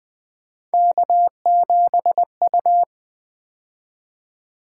Morse code, heard as a keyed tone, about 20 wpm, 700 Hz. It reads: K7U
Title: Three Letter-Number Combinations - 20wpm